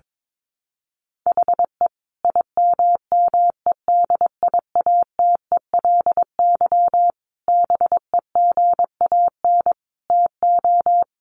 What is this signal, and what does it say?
Morse code, heard as a keyed tone, about 22 wpm, 700 Hz. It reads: HE IMMEDIATELY BEGAN TO
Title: Sentences from Top 1100 Words 22wpm